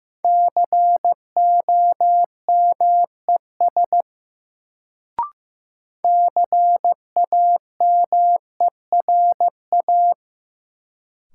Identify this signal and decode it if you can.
Morse code, heard as a keyed tone, about 15 wpm, 700 Hz. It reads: COMES CAMERA
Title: Sentences from Top 1000 Words - Review Part 5 - New Word Drill - 15wpm